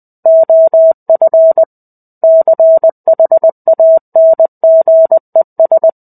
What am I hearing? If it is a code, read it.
Morse code, heard as a keyed tone, about 20 wpm, 650 Hz. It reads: OF CHANGES
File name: 05-Chapter-05 20 WPM